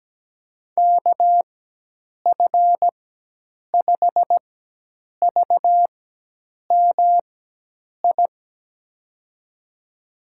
Morse code, heard as a keyed tone, about 17 wpm, spaced as for 8 wpm, 700 Hz. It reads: KF5VMI
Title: USA Call Signs - 6 Characters - 4x Character Spacing - 17wpm